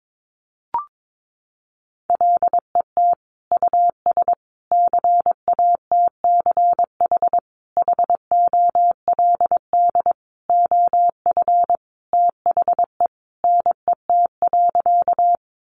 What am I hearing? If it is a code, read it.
Morse code, heard as a keyed tone, about 22 wpm, 700 Hz. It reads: LET US CATCH HOLD OF THE NET.
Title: Sentences from Top 1500 Words 22wpm